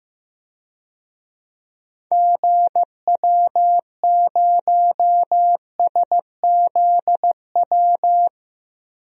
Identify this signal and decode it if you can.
Morse code, heard as a keyed tone, about 15 wpm, 700 Hz. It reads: GW0SZW
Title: DXCC-100 Call Signs - 6 Characters - 15wpm